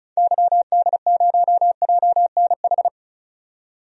Morse code, heard as a keyed tone, about 36 wpm, 700 Hz. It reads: YD0JNH
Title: DXCC-100 Call Signs - 6 Characters - 35wpm